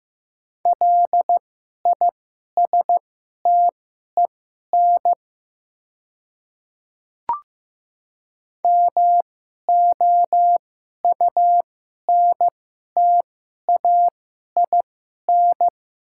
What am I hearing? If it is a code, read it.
Morse code, heard as a keyed tone, about 15 wpm, spaced as for 11 wpm, 700 Hz. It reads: LISTEN MOUNTAIN
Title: 2x Character Spacing - Top 500 Words - 15wpm